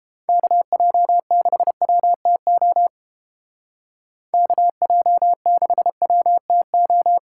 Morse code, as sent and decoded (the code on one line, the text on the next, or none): -.- .--- -.... .-- - --- / -.- .--- -.... .-- - ---
KJ6WTO KJ6WTO